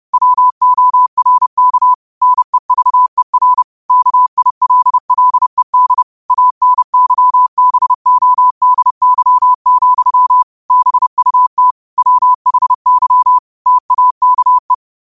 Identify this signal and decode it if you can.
Morse code, heard as a keyed tone, about 30 wpm, 1000 Hz. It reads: WORK NEVER KILLED ANYBODY, BUT WHY TAKE
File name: qotdfiles / 30wpm